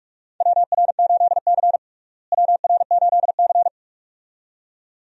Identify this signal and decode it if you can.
Morse code, heard as a keyed tone, about 46 wpm, 700 Hz. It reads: WR8C WR8C
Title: USA Call Signs - 4 Characters - Repeated 2x - 45wpm